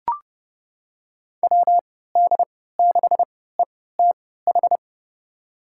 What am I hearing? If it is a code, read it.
Morse code, heard as a keyed tone, about 30 wpm, spaced as for 17 wpm, 700 Hz. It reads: WD6ETH
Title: USA Call Signs - 6 Characters - 3x Character Spacing - 30wpm